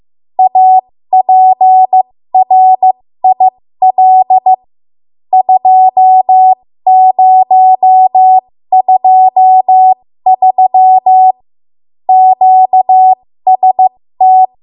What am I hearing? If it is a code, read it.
Morse code, heard as a keyed tone, about 15 wpm, spaced as for 13 wpm, 750 Hz. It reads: APRIL 2023 QST
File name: Morse / 13WPM